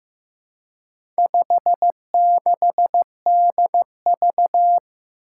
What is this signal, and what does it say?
Morse code, heard as a keyed tone, about 15 wpm, 700 Hz. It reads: 56DV
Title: Four Challenging Letter-Number Combinations 15wpm